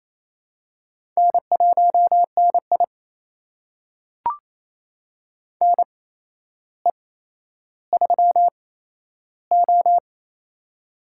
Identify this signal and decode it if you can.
Morse code, heard as a keyed tone, about 28 wpm, spaced as for 12 wpm, 700 Hz. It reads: N1NI NE3O
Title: USA Call Signs - 4 Characters - 8x Character Spacing - 28wpm